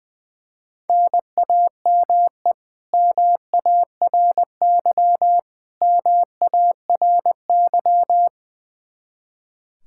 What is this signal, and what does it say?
Morse code, heard as a keyed tone, about 20 wpm, 700 Hz. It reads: NAME MARY MARY
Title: K1USN SST QSO Practice 20wpm